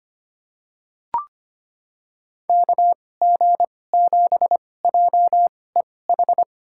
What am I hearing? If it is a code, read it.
Morse code, heard as a keyed tone, about 25 wpm, spaced as for 18 wpm, 690 Hz. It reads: KG7JEH
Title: USA Call Signs - 6 Characters - 2x Character Spacing - 25wpm